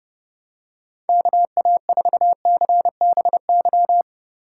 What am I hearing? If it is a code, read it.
Morse code, heard as a keyed tone, about 30 wpm, 700 Hz. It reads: KA4CBY
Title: USA Call Signs - 6 Characters - 4x Character Spacing - 30wpm